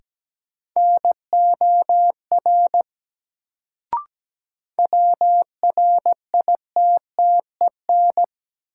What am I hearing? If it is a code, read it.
Morse code, heard as a keyed tone, about 17 wpm, 700 Hz. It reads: NOR WRITTEN
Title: Sentences from Top 1500 Words - New Word Drill - 17wpm